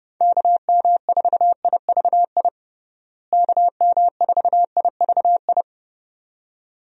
Morse code, heard as a keyed tone, about 30 wpm, 700 Hz. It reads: KM4IVI KM4IVI
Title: USA Call Signs - 6 Characters - Repeated 2x - 30wpm